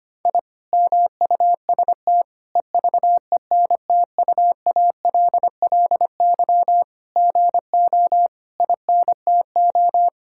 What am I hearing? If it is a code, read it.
Morse code, heard as a keyed tone, about 25 wpm, 700 Hz. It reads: I MUST EVENTUALLY GO INTO
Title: Sentences from Top 1000 Words - Review Part 6 - 25wpm